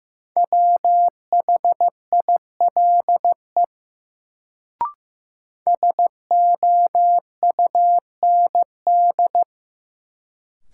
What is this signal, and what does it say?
Morse code, heard as a keyed tone, about 15 wpm, 700 Hz. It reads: WHILE SOUND
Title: Sets of 3 Words - New Word Drill - 15wpm